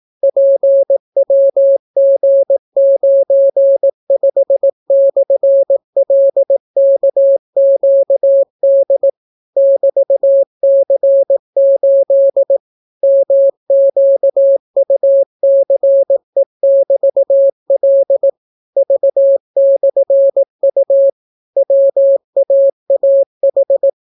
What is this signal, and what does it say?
Morse code, heard as a keyed tone, about 18 wpm, 550 Hz. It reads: PWG95/LKQD <BT>C8 MQUCE<BT>L V/U WAAH